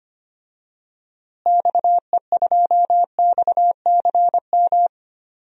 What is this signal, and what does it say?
Morse code, heard as a keyed tone, about 25 wpm, 700 Hz. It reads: XE2XCM
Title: DXCC-100 Call Signs - 6 Characters - Repeated 3x - 25wpm